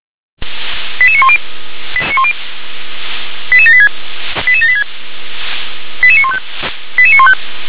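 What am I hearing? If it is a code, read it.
Five-tone selective call (ZVEI), heard as quick runs of five tones, tones of about 80 ms. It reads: 88910 88786 88786 88915 88915
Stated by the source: ZVEI five-tone selective call received on SDR.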